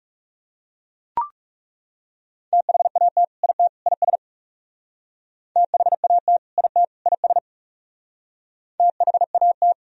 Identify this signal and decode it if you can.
Morse code, heard as a keyed tone, about 37 wpm, 700 Hz. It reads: THAT IT IS THAT IT IS THAT
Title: Speed-Racing - Sets of 3 Words 30wpm